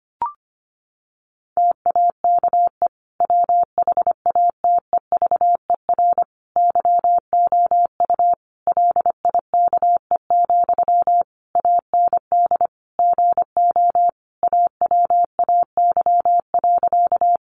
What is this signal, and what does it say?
Morse code, heard as a keyed tone, about 25 wpm, 700 Hz. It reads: TAKE WHATEVER YOU LIKE, AND GO AWAY.
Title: Sentences from Top 1100 Words 25wpm